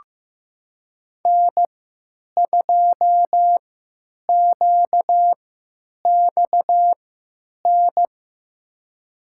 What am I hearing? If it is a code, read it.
Morse code, heard as a keyed tone, about 15 wpm, spaced as for 9 wpm, 700 Hz. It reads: N2QXN
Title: USA Call Signs - 5 Characters - 3x Character Spacing - 15wpm